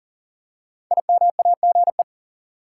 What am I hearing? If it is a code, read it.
Morse code, heard as a keyed tone, about 38 wpm, 700 Hz. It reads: IMAGE
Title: Five Letter Words - 40wpm